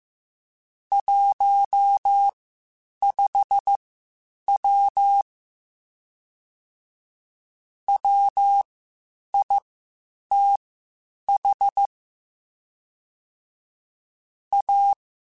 Morse code, heard as a keyed tone, about 15 wpm, spaced as for 8 wpm, 800 Hz. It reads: 15W WITH A
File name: (spaced) / QSOs Spaced-mp3s / WD0BGP 1 @13 sp